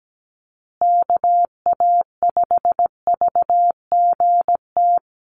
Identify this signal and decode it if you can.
Morse code, heard as a keyed tone, about 17 wpm, 700 Hz. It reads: KA5VGT
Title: USA Call Signs Repeated 3x 17wpm